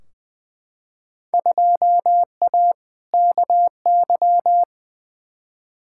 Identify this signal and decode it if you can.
Morse code, heard as a keyed tone, about 20 wpm, 700 Hz. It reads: 2A KY
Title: Field Day Exchanges - 20wpm